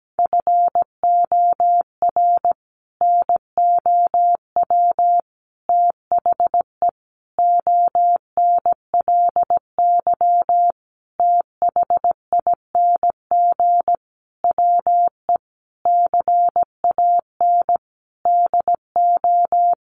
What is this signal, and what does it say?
Morse code, heard as a keyed tone, about 17 wpm, 700 Hz. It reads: FOR NOW THE ONLY THING WE CAN DO
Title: Sentences from Top 200 Words 17wpm